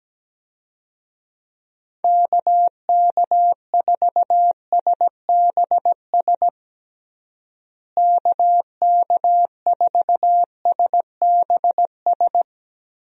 Morse code, heard as a keyed tone, about 17 wpm, 700 Hz. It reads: KK4SBS KK4SBS
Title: USA Call Signs - 6 Characters - Repeated 3x - 17wpm